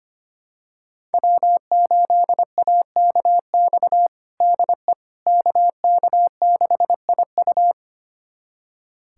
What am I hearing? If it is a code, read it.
Morse code, heard as a keyed tone, about 25 wpm, 700 Hz. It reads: W8AKX DE KK6IU
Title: USA Callsign DE Callsign - 5 Characters - 25wpm